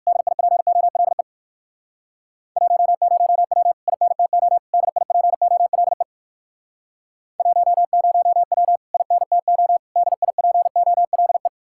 Morse code, heard as a keyed tone, about 59 wpm, 700 Hz. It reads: DIPOLE 10W INTO DIPOLE 10W INTO DIPOLE